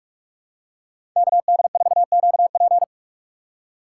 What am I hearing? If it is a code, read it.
Morse code, heard as a keyed tone, about 40 wpm, 700 Hz. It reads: KD4QP